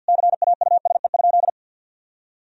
Morse code, heard as a keyed tone, about 47 wpm, 700 Hz. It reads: CAUSE?
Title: Sentences from Top 1000 Words - Review Part 3 - 50wpm